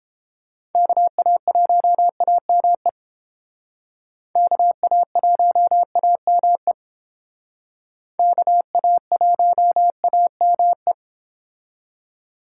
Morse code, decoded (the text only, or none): KA1AME KA1AME KA1AME